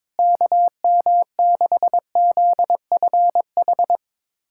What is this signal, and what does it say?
Morse code, heard as a keyed tone, about 22 wpm, 700 Hz. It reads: KM6ZFH
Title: USA Callsign DE Callsign - 6 Characters - 22wpm